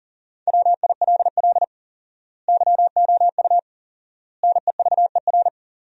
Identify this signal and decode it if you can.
Morse code, heard as a keyed tone, about 40 wpm, 700 Hz. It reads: WILL YOU NEVER
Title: 4x Word-Spacing - Sentences from Top 300 Words - 40wpm